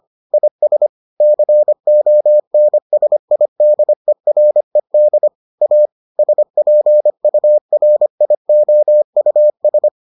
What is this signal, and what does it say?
Morse code, heard as a keyed tone, about 25 wpm, 600 Hz. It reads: IS CONSIDERED A SPURIOUS